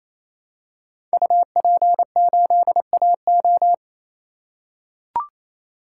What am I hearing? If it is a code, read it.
Morse code, heard as a keyed tone, about 28 wpm, 700 Hz. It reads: UP8AO
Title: Code Groups - 5 Letter:Numbers - 28wpm